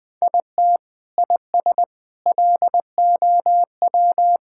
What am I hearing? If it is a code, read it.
Morse code, heard as a keyed tone, about 20 wpm, 700 Hz. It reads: IT IS LOW
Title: Sentences from Top 1000 Words - Review Part 5 - 20wpm